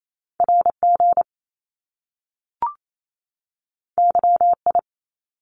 Morse code, heard as a keyed tone, about 28 wpm, 700 Hz. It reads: R G Y I
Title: Top Two Letter Combinations in Words 28wpm